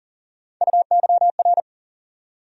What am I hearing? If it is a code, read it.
Morse code, heard as a keyed tone, about 39 wpm, 700 Hz. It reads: UYR
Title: USA Call Sign Postfixes - 3 Letters 40wpm